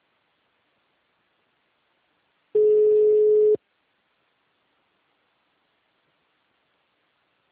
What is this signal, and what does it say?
Ringback tone (Europe, 425 Hz).